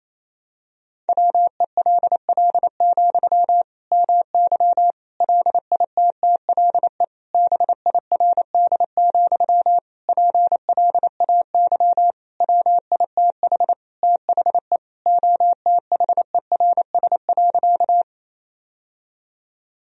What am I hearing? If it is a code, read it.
Morse code, heard as a keyed tone, about 28 wpm, 700 Hz. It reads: WELL, MY LITTLE BIRD, PLAY WITH THE OTHERS.
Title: Sentences from Top 1000 Words - Review Part 2 - 28wpm